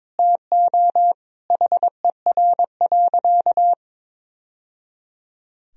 Morse code, heard as a keyed tone, about 22 wpm, 700 Hz. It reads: TO HER.
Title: Sentences from Top 1000 Words - Review Part 3 - 22wpm